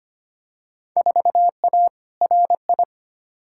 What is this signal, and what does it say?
Morse code, heard as a keyed tone, about 25 wpm, 700 Hz. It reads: 4A RI
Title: Field Day Exchanges Encore - 25wpm